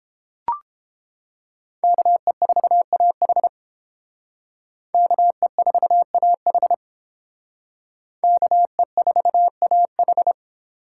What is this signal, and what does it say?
Morse code, heard as a keyed tone, about 29 wpm, 700 Hz. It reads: KE4AH KE4AH KE4AH